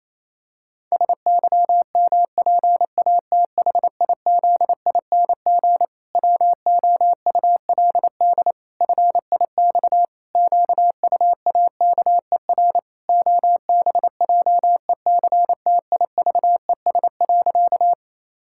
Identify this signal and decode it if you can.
Morse code, heard as a keyed tone, about 28 wpm, 700 Hz. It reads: SYMPATHIZING WOULD FIX QUAKER OBJECTIVES.
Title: Pangram Challenge v2 - 28wpm